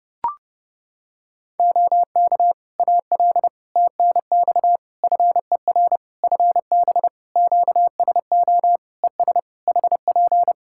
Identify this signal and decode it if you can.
Morse code, heard as a keyed tone, about 30 wpm, 700 Hz. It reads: OK AL TNX FER FB QSO ES HP
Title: Practice QSO - Exchange 4 - Ending - 30wpm